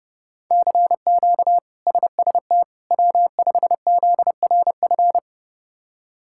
Morse code, heard as a keyed tone, about 32 wpm, 700 Hz. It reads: CQ SST W5ZRF
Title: K1USN SST QSO Practice - 30wpm